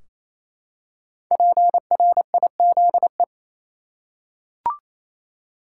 Morse code, heard as a keyed tone, about 28 wpm, 700 Hz. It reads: PRIZE
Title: Five Letter Words - 28wpm